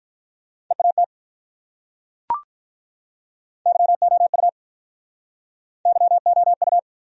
Morse code, heard as a keyed tone, about 45 wpm, 700 Hz. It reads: EAT YOU YOU